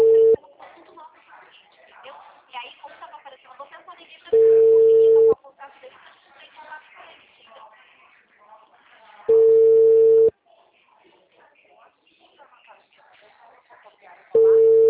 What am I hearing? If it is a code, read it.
Ringback tone (China, 450 Hz), with speech.
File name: out- -2012- - - .93450